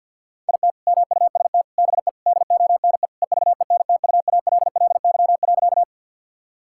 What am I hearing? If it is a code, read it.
Morse code, heard as a keyed tone, about 50 wpm, 700 Hz. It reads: IT MUST BE DONE EVENTUALLY.
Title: Sentences from Top 1000 Words - Review Part 6 - 50wpm